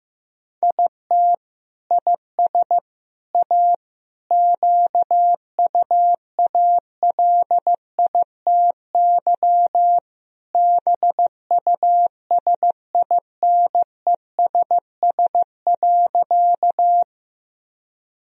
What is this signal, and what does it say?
Morse code, heard as a keyed tone, about 15 wpm, 700 Hz. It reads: IT IS A QUALITY BUSINESS.
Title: Sentences from Top 1000 Words - Review Part 4 - 15wpm